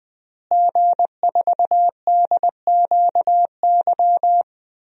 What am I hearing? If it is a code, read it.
Morse code, heard as a keyed tone, about 20 wpm, 700 Hz. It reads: G4DQY
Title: DXCC-100 Call Signs - 5 Characters - Repeated 2x - 20wpm